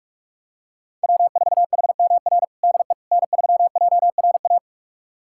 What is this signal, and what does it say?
Morse code, heard as a keyed tone, about 41 wpm, 700 Hz. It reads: W4HMR DE N3JRA
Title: USA Callsign DE Callsign - 5 Characters - 45wpm